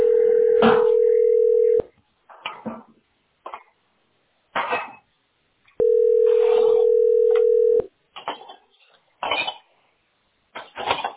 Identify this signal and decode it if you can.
Ringback tone (North America), 440 + 480 Hz.